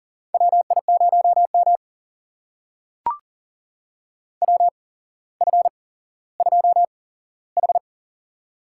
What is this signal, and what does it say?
Morse code, heard as a keyed tone, about 42 wpm, spaced as for 17 wpm, 700 Hz. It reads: WI0M WF2H
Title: USA Call Signs - 4 Characters - 8x Character Spacing - 40wpm